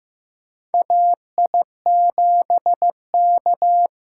Morse code, heard as a keyed tone, about 15 wpm, 700 Hz. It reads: AI7K
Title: USA Callsign DE Callsign - 4 Characters - 15wpm